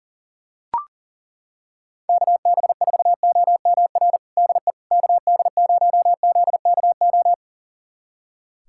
Morse code, heard as a keyed tone, about 41 wpm, 690 Hz. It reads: KB4OMR DE KD0ZKO
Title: USA Callsign DE Callsign - 6 Characters - 40wpm